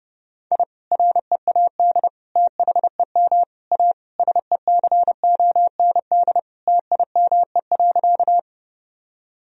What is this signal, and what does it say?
Morse code, heard as a keyed tone, about 30 wpm, 700 Hz. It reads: I READ THEM A SECOND TIME.
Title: Sentences from Top 1000 Words - Review Part 1 - 30wpm